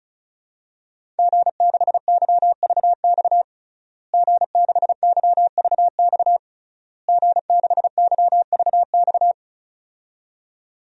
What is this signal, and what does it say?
Morse code, heard as a keyed tone, about 36 wpm, 700 Hz. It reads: G6YVX G6YVX G6YVX
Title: DXCC-100 Call Signs - 5 Characters - Repeated 3x - 35wpm